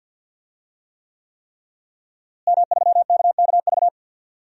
Morse code, heard as a keyed tone, about 49 wpm, 700 Hz. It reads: M3KKV
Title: DXCC-100 Call Signs - 5 Characters - Repeated 3x - 50wpm